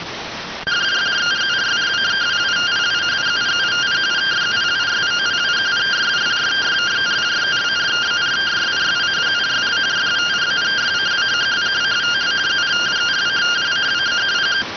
Two-tone teleprinter signal, about 170 Hz shift.